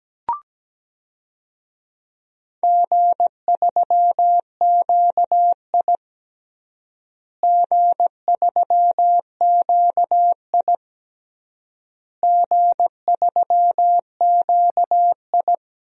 Morse code, heard as a keyed tone, about 17 wpm, 700 Hz. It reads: G3QI G3QI G3QI
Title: DXCC-100 Call Signs - 4 Characters - Repeated 3x - 17wpm